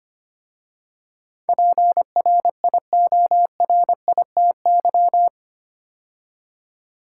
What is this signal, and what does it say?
Morse code, heard as a keyed tone, about 25 wpm, 700 Hz. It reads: PRIORITY
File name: 8 Letter Words - First 3 Letters Given - 25wpm